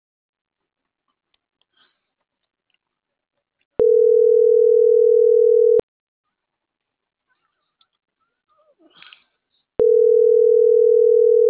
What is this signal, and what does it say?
Ringback tone (North America), 440 + 480 Hz.